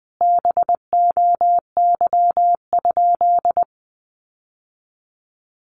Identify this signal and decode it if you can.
Morse code, heard as a keyed tone, about 20 wpm, 700 Hz. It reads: BOY?
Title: Sentences from Top 1000 Words 20wpm